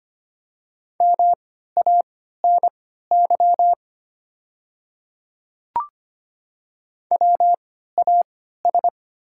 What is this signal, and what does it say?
Morse code, heard as a keyed tone, about 25 wpm, spaced as for 14 wpm, 700 Hz. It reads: MANY WAS
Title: 3x Character Spacing - Top 100 Words - 25wpm